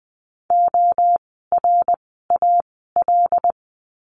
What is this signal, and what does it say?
Morse code, heard as a keyed tone, about 20 wpm, spaced as for 15 wpm, 700 Hz. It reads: ORAL
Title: 075 Lesson 13f - Letter L - Words 20wpm